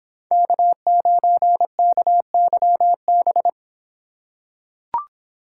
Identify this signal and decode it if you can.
Morse code, heard as a keyed tone, about 24 wpm, 700 Hz. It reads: K9KYB